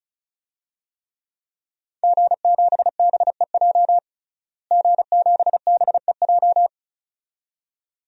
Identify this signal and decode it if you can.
Morse code, heard as a keyed tone, about 34 wpm, 700 Hz. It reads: G7BEJ G7BEJ
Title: DXCC-100 Call Signs - 5 Characters - Repeated 2x - 35wpm